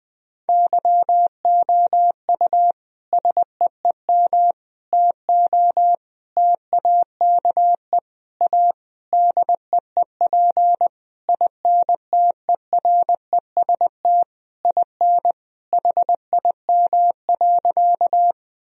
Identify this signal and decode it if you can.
Morse code, heard as a keyed tone, about 21 wpm, 700 Hz. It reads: YOU SEEM TO TAKE A DEEP INTEREST IN HIM.
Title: Sentences from Top 1000 Words - Review Part 3 - 20wpm